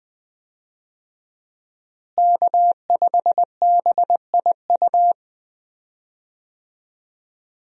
Morse code, heard as a keyed tone, about 21 wpm, 700 Hz. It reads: K5BIU
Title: USA Call Signs - 5 Characters - Repeated 2x - 20wpm